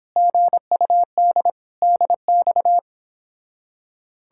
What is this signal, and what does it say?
Morse code, heard as a keyed tone, about 29 wpm, 700 Hz. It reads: GUD DX